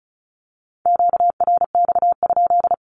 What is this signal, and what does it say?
Morse code, heard as a keyed tone, about 34 wpm, 700 Hz. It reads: QRX?
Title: Q-Codes 35wpm